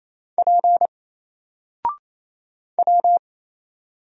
Morse code, heard as a keyed tone, about 28 wpm, 700 Hz. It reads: P W
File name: Single-Letter-Number-28wpm